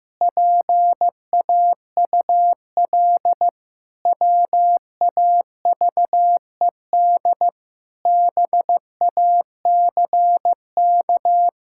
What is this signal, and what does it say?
Morse code, heard as a keyed tone, about 15 wpm, 700 Hz. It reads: PAUL WAVED BACK
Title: Pangram Challenge v2 - 15wpm